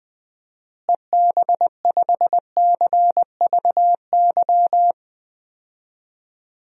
Morse code, heard as a keyed tone, about 21 wpm, 700 Hz. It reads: EB5CVY
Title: DXCC-100 Call Signs - 6 Characters - Repeated 2x - 20wpm